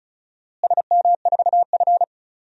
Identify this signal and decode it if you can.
Morse code, heard as a keyed tone, about 34 wpm, 700 Hz. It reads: SM4F
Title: DXCC-100 Call Signs - 4 Characters - Repeated 3x - 35wpm